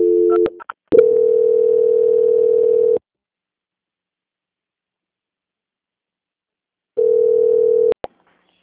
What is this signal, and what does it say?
Ringback tone (North America), 440 + 480 Hz.
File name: out- -2005- - - .4